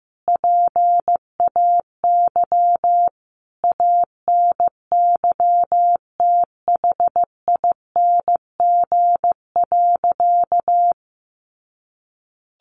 Morse code, heard as a keyed tone, about 14 wpm, 700 Hz. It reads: PAY ANYTHING.
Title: A Sound Of Thunder 15wpm